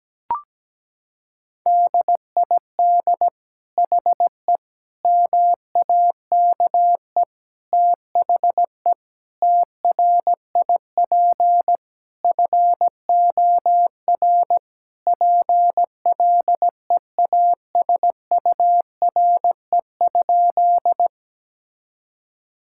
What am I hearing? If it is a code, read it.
Morse code, heard as a keyed tone, about 17 wpm, 700 Hz. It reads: DID HE MAKE THE TRIP FOR PLEASURE?
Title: Sentences from Top 1600 Words 17wpm